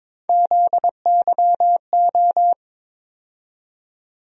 Morse code, heard as a keyed tone, about 22 wpm, 700 Hz. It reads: ZYO
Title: USA Call Sign Postfixes - 3 Letters 22wpm